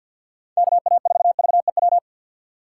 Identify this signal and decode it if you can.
Morse code, heard as a keyed tone, about 52 wpm, 700 Hz. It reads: KA4VEW
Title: USA Call Signs - 6 Characters - 4x Character Spacing - 50wpm